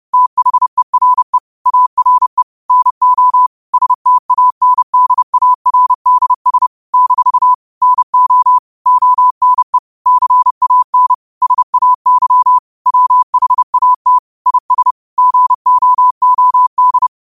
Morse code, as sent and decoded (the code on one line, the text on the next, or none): - .... .-. / .- .-. / -. --- / ... - .- -. -.. .- .-. -.. ... / -....- / -. --- / --- -. / -.-. .- -. / ... .- -.-- / .-- .... .- - / .. ... / --. --- --- -..
THERE ARE NO STANDARDS - NO ONE CAN SAY WHAT IS GOOD